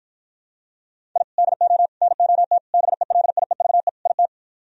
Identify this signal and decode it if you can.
Morse code, heard as a keyed tone, about 53 wpm, 700 Hz. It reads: I DO NOT BELIEVE IT